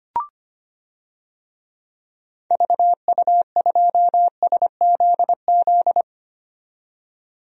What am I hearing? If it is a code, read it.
Morse code, heard as a keyed tone, about 25 wpm, 700 Hz. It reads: VU2SZZ
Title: DXCC-100 Call Signs - 6 Characters - 25wpm